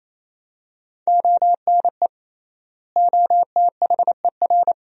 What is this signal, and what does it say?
Morse code, heard as a keyed tone, about 28 wpm, 700 Hz. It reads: ONE OTHER
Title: 3x Word-Spacing - Sentences from Top 400 Words - 28wpm